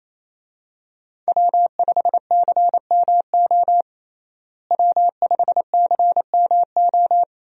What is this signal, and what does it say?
Morse code, heard as a keyed tone, about 28 wpm, 700 Hz. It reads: W5CMO W5CMO
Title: USA Call Signs - 5 Characters - Repeated 2x - 28wpm